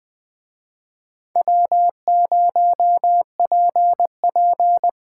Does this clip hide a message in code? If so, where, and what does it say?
Yes: Morse code from 1.4 to 4.9 s, heard as a keyed tone, about 20 wpm, 700 Hz. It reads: W0PP